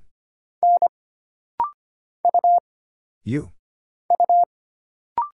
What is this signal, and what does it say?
Morse code, heard as a keyed tone, about 25 wpm, 700 Hz. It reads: N U U
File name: Single Character (Letter:Number:Punctuation:Prosign) - Rapid-Fire - 25wpm